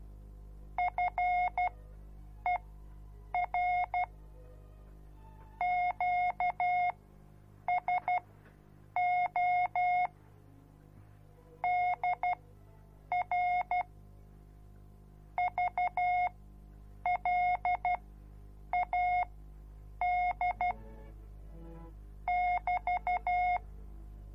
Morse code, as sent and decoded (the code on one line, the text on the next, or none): ..-. .-. / --.- ... --- / -.. .-. / ...- .-.. .- -.. / -...-
FER QSO DR VLAD <BT>